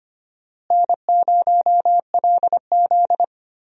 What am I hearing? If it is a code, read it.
Morse code, heard as a keyed tone, about 25 wpm, 700 Hz. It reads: N0LZ
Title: USA Call Signs - 4 Characters - Repeated 2x - 25wpm